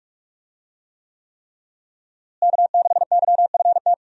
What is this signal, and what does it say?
Morse code, heard as a keyed tone, about 46 wpm, 700 Hz. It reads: K6YFT
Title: USA Call Signs - 5 Characters - Repeated 3x - 45wpm